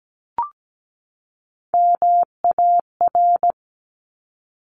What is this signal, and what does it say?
Morse code, heard as a keyed tone, about 17 wpm, 700 Hz. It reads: MAR
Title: Top Three Letter Combinations in Words 17wpm